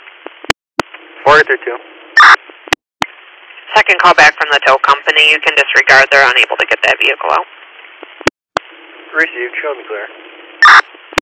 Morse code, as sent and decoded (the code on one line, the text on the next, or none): none